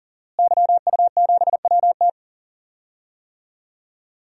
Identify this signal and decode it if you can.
Morse code, heard as a keyed tone, about 38 wpm, 700 Hz. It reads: YU7WT
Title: DXCC-100 Call Signs - 5 Characters - Repeated 2x - 40wpm